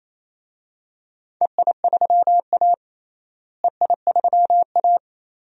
Morse code, heard as a keyed tone, about 28 wpm, 700 Hz. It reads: EI3A EI3A
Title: DXCC-100 Call Signs - 4 Characters - Repeated 2x - 28wpm